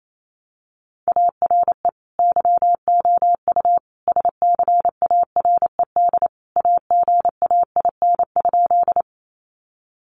Morse code, heard as a keyed tone, about 28 wpm, 700 Hz. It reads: ARE YOU SCARED AGAIN?
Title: Sentences from Top 700 Words 28wpm